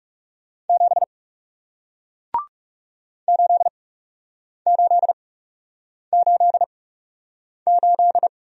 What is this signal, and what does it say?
Morse code, heard as a keyed tone, about 41 wpm, spaced as for 8 wpm, 700 Hz. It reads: 7 8888